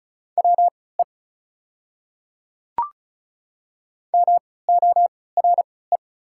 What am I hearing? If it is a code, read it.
Morse code, heard as a keyed tone, about 36 wpm, spaced as for 20 wpm, 690 Hz. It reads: WE MORE
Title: 3x Character Spacing - Top 500 Words - 35wpm